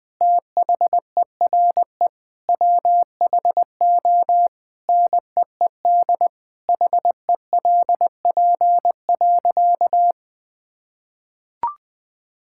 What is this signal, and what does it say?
Morse code, heard as a keyed tone, about 20 wpm, 700 Hz. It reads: THERE WHO NEED HELP.
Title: Sentences with 8-Word Phrases - 20wpm